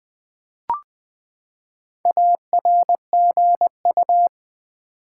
Morse code, heard as a keyed tone, about 20 wpm, 700 Hz. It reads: ARGU
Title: Build A Word - 4 Letters At A Time - 8-Letter Words - 20wpm